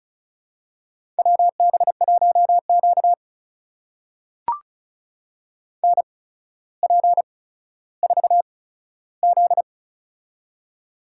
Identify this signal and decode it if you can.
Morse code, heard as a keyed tone, about 36 wpm, spaced as for 15 wpm, 700 Hz. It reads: WB1Q NP4Z
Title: USA Call Signs - 4 Characters - 8x Character Spacing - 35wpm